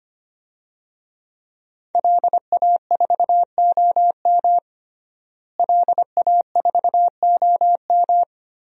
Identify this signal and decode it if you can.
Morse code, heard as a keyed tone, about 25 wpm, 700 Hz. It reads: LA4OM LA4OM
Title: DXCC-100 Call Signs - 5 Characters - Repeated 2x - 25wpm